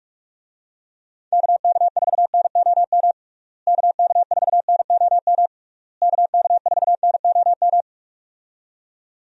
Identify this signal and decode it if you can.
Morse code, heard as a keyed tone, about 46 wpm, 700 Hz. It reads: KK4NOM KK4NOM KK4NOM
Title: USA Call Signs - 6 Characters - Repeated 3x - 45wpm